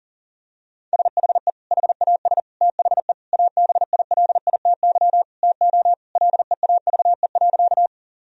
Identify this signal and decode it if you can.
Morse code, heard as a keyed tone, about 40 wpm, 700 Hz. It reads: SHE HAS THE ABILITY TO LEAVE.
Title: Sentences from Top 1000 Words - Review Part 3 - 40wpm